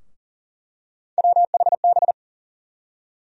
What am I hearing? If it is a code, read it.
Morse code, heard as a keyed tone, about 40 wpm, 700 Hz. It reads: WHB
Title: USA Call Sign Postfixes - 3 Letters 40wpm